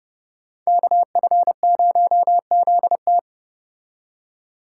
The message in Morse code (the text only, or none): KF0ZT